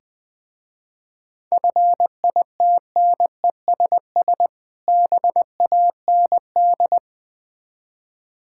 Morse code, heard as a keyed tone, about 21 wpm, 700 Hz. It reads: FITNESS BAND
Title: Difficult Words with Hints - 20wpm